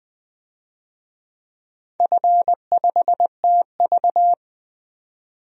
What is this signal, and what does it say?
Morse code, heard as a keyed tone, about 20 wpm, 700 Hz. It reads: F5TV
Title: DXCC-100 Call Signs - 4 Characters - 20wpm